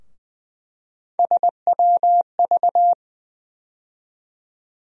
Morse code, heard as a keyed tone, about 20 wpm, 700 Hz. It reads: SWV
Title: USA Call Sign Postfixes - 3 Letters 20wpm